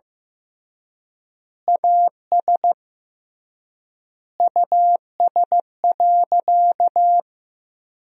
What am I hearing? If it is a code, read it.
Morse code, heard as a keyed tone, about 15 wpm, 700 Hz. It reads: AS US.
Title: 3x Word-Spacing - Sentences from Top 500 Words - 15wpm